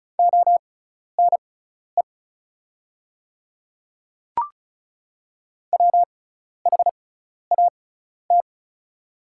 Morse code, heard as a keyed tone, about 36 wpm, spaced as for 12 wpm, 700 Hz. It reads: ONE WHAT
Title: 6x Character Spacing - Top 100 Words - 35wpm